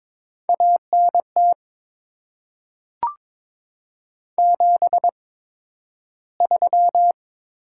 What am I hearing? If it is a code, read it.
Morse code, heard as a keyed tone, about 22 wpm, spaced as for 12 wpm, 700 Hz. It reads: ANT 73
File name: Top 90 QSO Elements - 8x Character Spacing - 22wpm